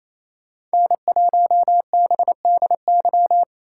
Morse code, heard as a keyed tone, about 36 wpm, 700 Hz. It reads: N1BDY